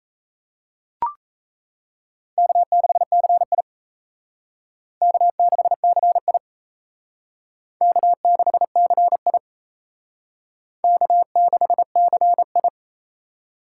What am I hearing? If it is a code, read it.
Morse code, heard as a keyed tone, about 36 wpm, 700 Hz. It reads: K6CI K6CI K6CI K6CI